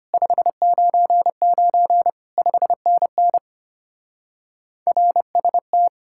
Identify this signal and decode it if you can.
Morse code, heard as a keyed tone, about 29 wpm, 700 Hz. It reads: 599 5NN RST
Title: Speed-Racing Common QSO Phrases Encore - 25wpm